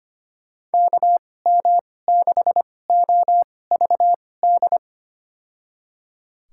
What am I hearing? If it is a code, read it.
Morse code, heard as a keyed tone, about 25 wpm, spaced as for 18 wpm, 700 Hz. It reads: KM6OVD
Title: USA Call Signs - 6 Characters - 2x Character Spacing - 25wpm